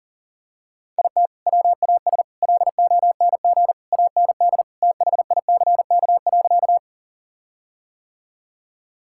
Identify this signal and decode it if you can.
Morse code, heard as a keyed tone, about 40 wpm, 700 Hz. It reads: IT WAS LONG AND THICK.
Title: Sentences from Top 1600 Words 40wpm